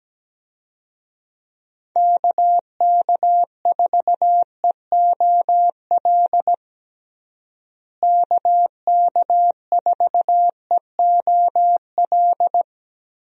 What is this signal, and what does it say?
Morse code, heard as a keyed tone, about 17 wpm, 700 Hz. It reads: KK4EOL KK4EOL
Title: USA Call Signs - 6 Characters - Repeated 3x - 17wpm